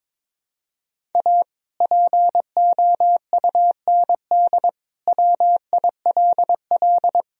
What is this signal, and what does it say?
Morse code, heard as a keyed tone, about 22 wpm, 700 Hz. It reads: A POUND WILL
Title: Sentences from Top 1000 Words - Review Part 2 - 22wpm